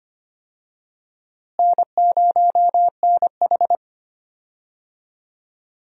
Morse code, heard as a keyed tone, about 25 wpm, 700 Hz. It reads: N0NH
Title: USA Call Signs - 4 Characters - Repeated 2x - 25wpm